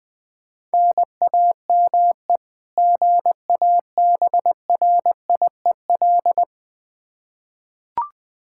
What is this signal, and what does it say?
Morse code, heard as a keyed tone, about 20 wpm, 700 Hz. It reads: NAME GABRIEL
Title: K1USN SST QSO Practice 20wpm